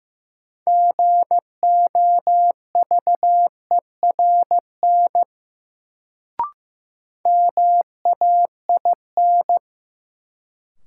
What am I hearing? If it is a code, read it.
Morse code, heard as a keyed tone, about 16 wpm, 700 Hz. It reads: GOVERN MAIN
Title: Sentences from Top 1000 Words - Review Part 2 - New Word Drill - 15wpm